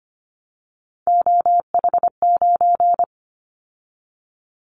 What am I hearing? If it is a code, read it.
Morse code, heard as a keyed tone, about 25 wpm, 700 Hz. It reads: OH9
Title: Callsign Prefixes for Top 25 DXCC - 25wpm